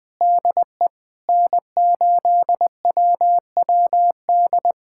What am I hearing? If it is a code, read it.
Morse code, heard as a keyed tone, about 20 wpm, 700 Hz. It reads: DE N8WWD
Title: USA Callsign DE Callsign 20wpm